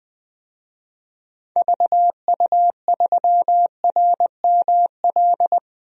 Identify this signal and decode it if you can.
Morse code, heard as a keyed tone, about 20 wpm, 700 Hz. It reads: VU3RML
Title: DXCC-100 Call Signs - 6 Characters - 20wpm